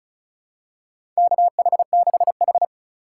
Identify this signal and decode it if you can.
Morse code, heard as a keyed tone, about 36 wpm, 700 Hz. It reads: KH6H
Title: USA Call Signs - 3x Character Spacing - 35wpm